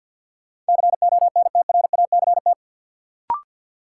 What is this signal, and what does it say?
Morse code, heard as a keyed tone, about 53 wpm, 700 Hz. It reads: CONTRACT
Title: Sentences from Top 1400 Words - New Word Drill Alt - 50wpm